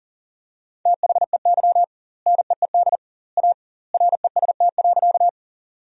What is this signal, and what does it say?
Morse code, heard as a keyed tone, about 40 wpm, 700 Hz. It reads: THEY NEED A REST.
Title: 2x Word-Spacing - Sentences from Top 500 Words - 40wpm